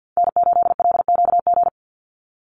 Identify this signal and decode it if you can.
Morse code, heard as a keyed tone, about 48 wpm, 700 Hz. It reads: N8LQG
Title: USA Call Signs 50wpm